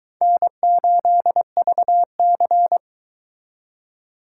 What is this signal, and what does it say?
Morse code, heard as a keyed tone, about 22 wpm, 700 Hz. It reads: N8VC